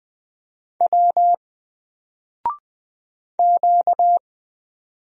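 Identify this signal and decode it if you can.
Morse code, heard as a keyed tone, about 20 wpm, 700 Hz. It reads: W Q
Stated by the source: Single letter and number Morse drill at 20 wpm, from N4LCD.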